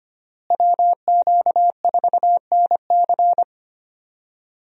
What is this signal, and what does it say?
Morse code, heard as a keyed tone, about 25 wpm, 700 Hz. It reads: WQ4NC
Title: USA Call Signs - 5 Characters - Repeated 3x - 25wpm